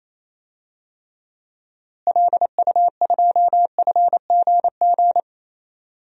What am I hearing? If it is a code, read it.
Morse code, heard as a keyed tone, about 28 wpm, 700 Hz. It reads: LU2FGG
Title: DXCC-100 Call Signs - 6 Characters - Repeated 2x - 28wpm